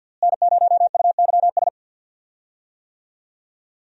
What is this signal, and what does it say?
Morse code, heard as a keyed tone, about 51 wpm, 700 Hz. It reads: N0UYS
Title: USA Call Signs - 5 Characters - Repeated 3x - 50wpm